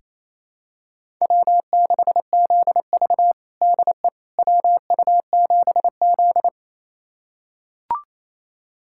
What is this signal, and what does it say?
Morse code, heard as a keyed tone, about 27 wpm, 700 Hz. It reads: W6ZV DE WU7Z
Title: USA Callsign DE Callsign - 4 Characters - 28wpm